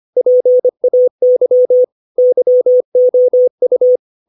Morse code, heard as a keyed tone, about 25 wpm, 500 Hz. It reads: PAY YOU
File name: 632 words / 632 Words 25WPM